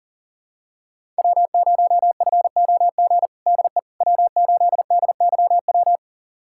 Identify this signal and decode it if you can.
Morse code, heard as a keyed tone, about 40 wpm, 700 Hz. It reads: W0FOG DE W8DYW
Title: USA Callsign DE Callsign - 5 Characters - 40wpm